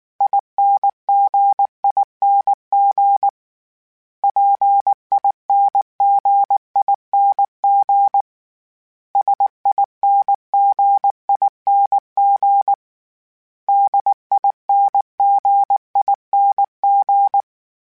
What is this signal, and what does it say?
Morse code, heard as a keyed tone, about 19 wpm, 800 Hz. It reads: INGING PINGING SINGING DINGING
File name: Special INGs-mp3s / Special INGs-mp3s / Dble ings @18